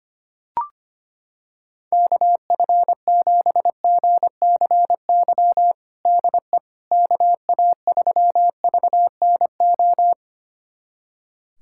Morse code, heard as a keyed tone, about 25 wpm, 700 Hz. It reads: KF7GCY DE KA3VNO
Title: USA Callsign DE Callsign - 6 Characters - 25wpm